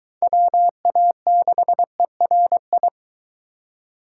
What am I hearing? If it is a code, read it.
Morse code, heard as a keyed tone, about 22 wpm, 700 Hz. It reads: WA6ERI